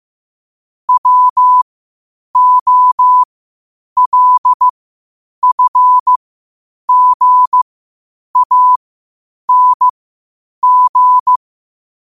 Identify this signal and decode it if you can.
Morse code, heard as a keyed tone, about 15 wpm, spaced as for 8 wpm, 1000 Hz. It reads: WOLFGANG
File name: qotdfiles / 8wpm